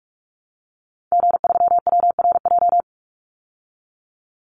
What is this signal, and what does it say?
Morse code, heard as a keyed tone, about 44 wpm, 700 Hz. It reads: G3WRJ
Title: Call Signs Repeated 3x - 45wpm